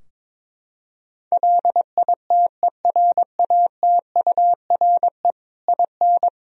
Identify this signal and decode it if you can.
Morse code, heard as a keyed tone, about 22 wpm, 700 Hz. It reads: LITERATURE IN
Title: Sentences from Top 1000 Words - Review Part 3 - 22wpm